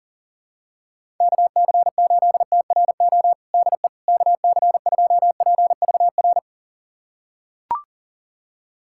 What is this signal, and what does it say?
Morse code, heard as a keyed tone, about 40 wpm, 700 Hz. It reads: KC8TRO DE KC2PVR
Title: USA Callsign DE Callsign - 6 Characters - 40wpm